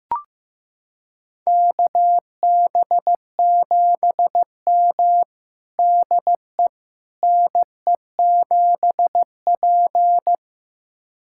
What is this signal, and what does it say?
Morse code, heard as a keyed tone, about 16 wpm, 700 Hz. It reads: KB7M DE NE7P
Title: USA Callsign DE Callsign - 4 Characters - 15wpm